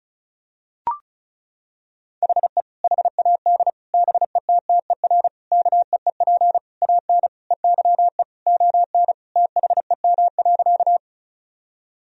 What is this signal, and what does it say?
Morse code, heard as a keyed tone, about 36 wpm, 700 Hz. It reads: HE HAD BETTER KEEP AN EYE ON THEM.
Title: Sentences from Top 1000 Words - Review Part 1 - 35wpm